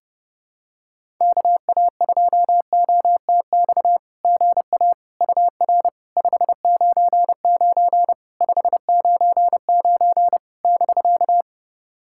Morse code, heard as a keyed tone, about 30 wpm, 700 Hz. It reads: KA2OTX GA UR 599 599 <BK>
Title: SOTA QSO Practice 30wpm